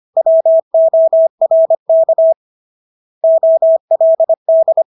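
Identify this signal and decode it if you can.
Morse code, heard as a keyed tone, about 25 wpm, 650 Hz. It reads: WORK OLD
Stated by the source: Morse practice of English words at 25 wpm from CW med Gnister.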